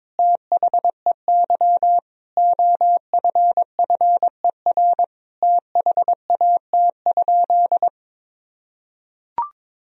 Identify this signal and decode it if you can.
Morse code, heard as a keyed tone, about 22 wpm, 700 Hz. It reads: THEY OFFER THAT?
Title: Sentences from Top 1000 Words - Review Part 4 - 22wpm